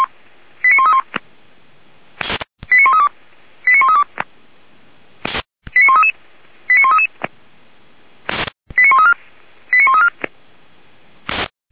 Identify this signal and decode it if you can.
Five-tone selective call (ZVEI), heard as quick runs of five tones, tones of about 70 ms. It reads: 89131 89132 89132 89133 89133 89134 89134